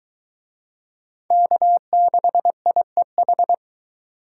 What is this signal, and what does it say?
Morse code, heard as a keyed tone, about 23 wpm, 700 Hz. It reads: K6IEH